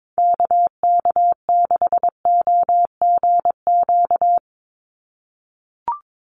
Morse code, heard as a keyed tone, about 22 wpm, 700 Hz. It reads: KK6OGQ
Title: USA Call Signs 22wpm